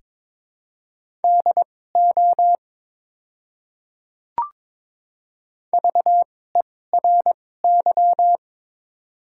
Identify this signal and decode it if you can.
Morse code, heard as a keyed tone, about 22 wpm, spaced as for 16 wpm, 700 Hz. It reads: DO VERY
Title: 2x Character Spacing - Top 300 Words - 22wpm